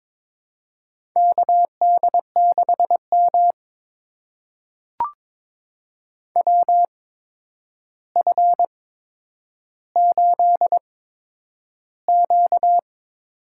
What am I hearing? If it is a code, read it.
Morse code, heard as a keyed tone, about 22 wpm, spaced as for 9 wpm, 700 Hz. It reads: KD6M WF8Q
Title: USA Call Signs - 4 Characters - 8x Character Spacing - 22wpm